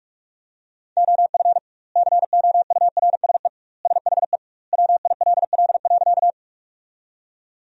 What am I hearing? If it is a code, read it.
Morse code, heard as a keyed tone, about 46 wpm, 700 Hz. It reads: OF COURSE SHE WILL.
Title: 2x Word-Spacing - Sentences from Top 500 Words - 45wpm